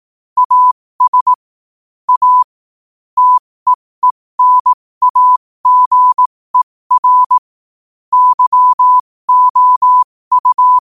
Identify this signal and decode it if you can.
Morse code, heard as a keyed tone, about 18 wpm, spaced as for 15 wpm, 1000 Hz. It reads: AS A TEENAGER YOU
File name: qotdfiles / 15wpm